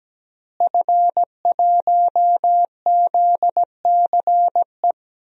Morse code, heard as a keyed tone, about 17 wpm, 700 Hz. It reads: F1ZCE
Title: DXCC-100 Call Signs - 5 Characters - Repeated 2x - 17wpm